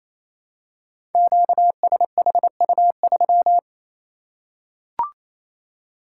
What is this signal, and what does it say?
Morse code, heard as a keyed tone, about 27 wpm, 700 Hz. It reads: QSHU3
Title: Code Groups - 5 Letter:Numbers - 28wpm